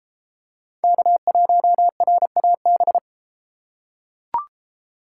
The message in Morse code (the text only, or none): K1RAB